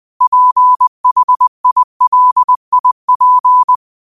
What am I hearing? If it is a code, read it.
Morse code, heard as a keyed tone, about 20 wpm, 1000 Hz. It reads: PHILIP
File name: qotdfiles / 20wpm